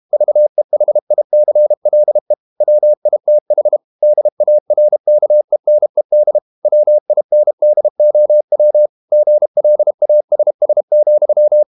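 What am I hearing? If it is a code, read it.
Morse code, heard as a keyed tone, about 32 wpm, 600 Hz. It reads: VEHICLE WITH DARKENED WINDOW GLASS,
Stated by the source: Morse code practice at 32 wpm from CW med Gnister, reading Ken Follett's novel Never in English.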